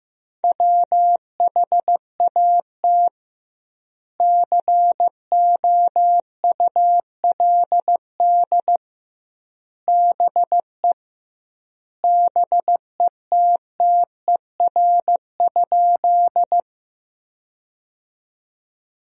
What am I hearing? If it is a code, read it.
Morse code, heard as a keyed tone, about 15 wpm, 700 Hz. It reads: WHAT COULD BE BETTER?
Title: 2x Word-Spacing - Sentences from Top 400 Words - 15wpm